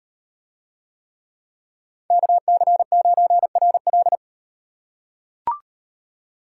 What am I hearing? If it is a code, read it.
Morse code, heard as a keyed tone, about 37 wpm, 700 Hz. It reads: KC9RL